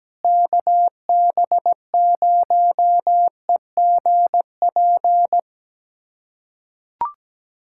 Morse code, heard as a keyed tone, about 17 wpm, 700 Hz. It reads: KB0EGP
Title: USA Callsign DE Callsign 17wpm